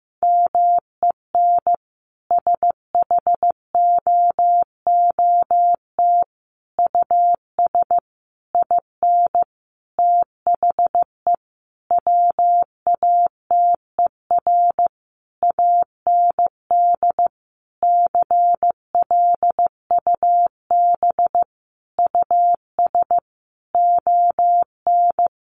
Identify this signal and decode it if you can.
Morse code, heard as a keyed tone, about 14 wpm, 700 Hz. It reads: MEN SHOOT US IN THE WATER AND CLUB US ON
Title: Sentences from Top 1200 Words 15wpm